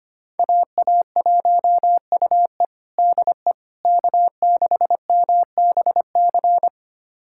Morse code, heard as a keyed tone, about 25 wpm, 700 Hz. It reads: AA1UE DE K6MBC
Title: USA Callsign DE Callsign - 5 Characters - 25wpm